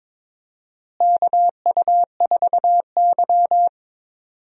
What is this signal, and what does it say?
Morse code, heard as a keyed tone, about 22 wpm, 700 Hz. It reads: KU4Y
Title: USA Call Signs - 4 Characters - Repeated 2x - 22wpm